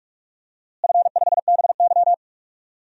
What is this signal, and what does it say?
Morse code, heard as a keyed tone, about 41 wpm, 700 Hz. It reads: F5BY
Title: DXCC-100 Call Signs - 4 Characters - Repeated 3x - 45wpm